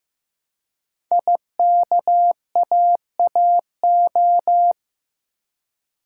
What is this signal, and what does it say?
Morse code, heard as a keyed tone, about 15 wpm, 700 Hz. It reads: IKAAO
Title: Code Groups - 5 Letters - 15wpm